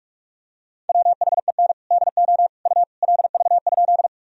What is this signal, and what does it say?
Morse code, heard as a keyed tone, about 46 wpm, 700 Hz. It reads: WHEN DO U LV?
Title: Sentences from Top 400 Words and Abbreviations 45wpm